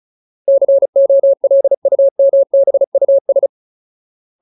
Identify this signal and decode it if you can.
Morse code, heard as a keyed tone, about 34 wpm, 550 Hz. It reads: COLUMBUS
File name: Ultimate Word List / 35WPM Ultimate